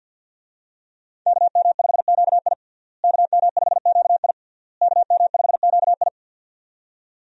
Morse code, heard as a keyed tone, about 52 wpm, 700 Hz. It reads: KM5QI KM5QI KM5QI
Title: USA Call Signs - 5 Characters - Repeated 3x - 50wpm